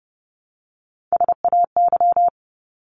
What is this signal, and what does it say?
Morse code, heard as a keyed tone, about 30 wpm, 700 Hz. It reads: SAY
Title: ES:S Suffix - Top 100 Words with Base Words 30wpm